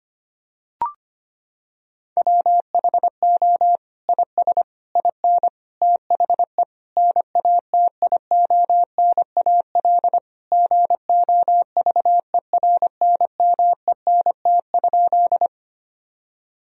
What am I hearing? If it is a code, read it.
Morse code, heard as a keyed tone, about 25 wpm, 700 Hz. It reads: WHO IS IN THE NATIONAL GOVERNMENT?
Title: Sentences from Top 1000 Words - Review Part 4 - 25wpm